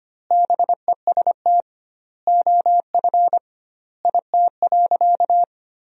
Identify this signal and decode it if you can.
Morse code, heard as a keyed tone, about 25 wpm, 700 Hz. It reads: BEST OF IT.
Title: 2x Word-Spacing - Sentences from Top 400 Words - 25wpm